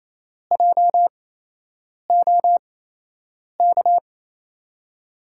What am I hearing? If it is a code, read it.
Morse code, heard as a keyed tone, about 28 wpm, spaced as for 8 wpm, 700 Hz. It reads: JOK
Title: USA Call Signs - 5 Characters - 8x Character Spacing - 28wpm